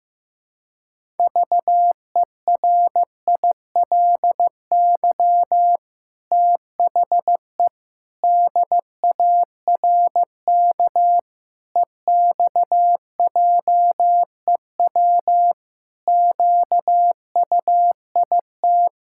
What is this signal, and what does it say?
Morse code, heard as a keyed tone, about 15 wpm, 700 Hz. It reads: VERILY THE DARK EXJEW QUIT
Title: Pangram Challenge v2 - 15wpm